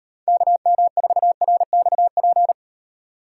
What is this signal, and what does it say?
Morse code, heard as a keyed tone, about 33 wpm, 700 Hz. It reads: KM4RXP